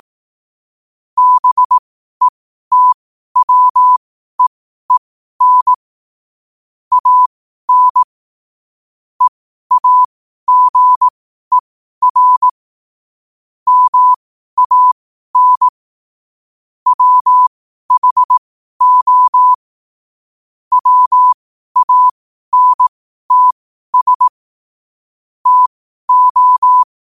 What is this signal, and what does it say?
Morse code, heard as a keyed tone, about 18 wpm, spaced as for 12 wpm, 1000 Hz. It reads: BETWEEN AN EAGER MAN WHO WANTS TO
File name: qotdfiles / 12wpm